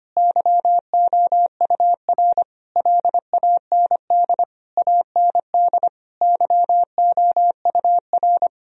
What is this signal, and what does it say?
Morse code, heard as a keyed tone, about 25 wpm, 700 Hz. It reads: YOUR LAND AND YOUR
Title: Sentences from Top 1000 Words - Review Part 4 - 25wpm